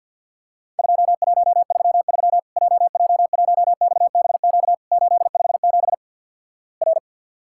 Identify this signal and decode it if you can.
Morse code, heard as a keyed tone, about 50 wpm, 700 Hz. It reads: 2133 JJ1XBQ 857 R
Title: Worked All Europe DX Contest - QTC Exchange with Repeats - 50wpm